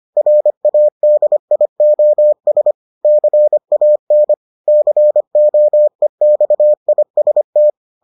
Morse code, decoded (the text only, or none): RADIOS CAN COEXIST